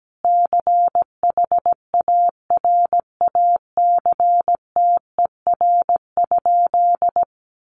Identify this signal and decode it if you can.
Morse code, heard as a keyed tone, about 17 wpm, 700 Hz. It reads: CHARACTER?
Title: Sentences from Top 1200 Words 17wpm